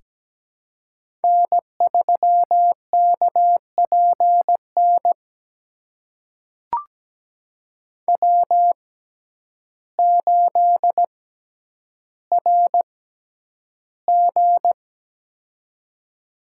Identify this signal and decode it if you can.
Morse code, heard as a keyed tone, about 17 wpm, spaced as for 9 wpm, 700 Hz. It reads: N3KPN W8RG
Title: USA Call Signs - 6x Character Spacing - 17wpm